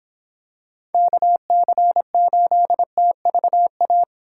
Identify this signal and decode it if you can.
Morse code, heard as a keyed tone, about 25 wpm, 700 Hz. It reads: KC8TVA